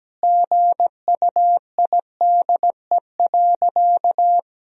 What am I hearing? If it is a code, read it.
Morse code, heard as a keyed tone, about 17 wpm, 700 Hz. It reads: GUIDE.
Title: Sentences from Top 1000 Words - Review Part 3 - 17wpm